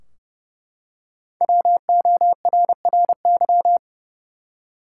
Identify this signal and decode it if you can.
Morse code, heard as a keyed tone, about 29 wpm, 700 Hz. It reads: WORRY
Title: Five Letter Words - 30wpm